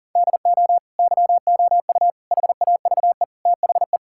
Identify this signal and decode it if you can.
Morse code, heard as a keyed tone, about 40 wpm, 700 Hz. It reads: DO YOU HAVE THE
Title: Sentences from Top 1000 Words - Review Part 5 - 40wpm